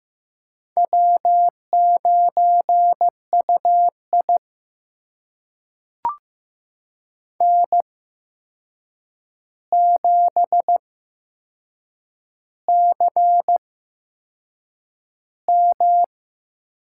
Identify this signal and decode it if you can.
Morse code, heard as a keyed tone, about 15 wpm, spaced as for 6 wpm, 700 Hz. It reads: W9UI N7CM
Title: USA Call Signs - 4 Characters - 8x Character Spacing - 15wpm